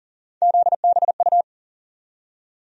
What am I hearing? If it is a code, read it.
Morse code, heard as a keyed tone, about 40 wpm, 700 Hz. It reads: ZBU
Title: USA Call Sign Postfixes - 3 Letters 40wpm